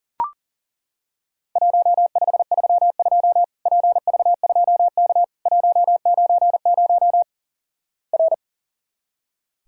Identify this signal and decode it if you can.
Morse code, heard as a keyed tone, about 37 wpm, 700 Hz. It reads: 1532 PV2K 190 R
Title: Worked All Europe DX Contest - QTC Exchange - 40wpm